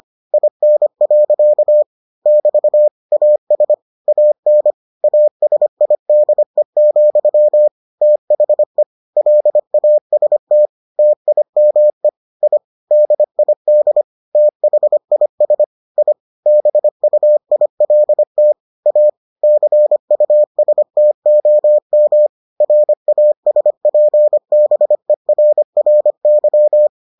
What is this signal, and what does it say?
Morse code, heard as a keyed tone, about 25 wpm, 600 Hz. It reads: IN. <BT> AS AN ASIDE, THE LAST TIME I DID THIS I BUILT A CUSTOM RASPBERRY